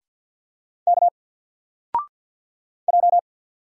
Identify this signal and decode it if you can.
Morse code, heard as a keyed tone, about 50 wpm, 700 Hz. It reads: K J
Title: Single Letter-Number 50wpm